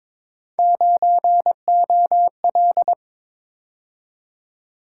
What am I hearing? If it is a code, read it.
Morse code, heard as a keyed tone, about 27 wpm, 700 Hz. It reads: 9OL
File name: Speed-Racing - Three Letter-Number Combinations - 22wpm